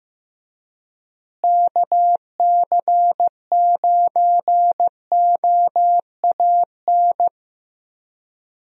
Morse code, heard as a keyed tone, about 15 wpm, 700 Hz. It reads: KC9OAN
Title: USA Call Signs - 6 Characters - Repeated 3x - 15wpm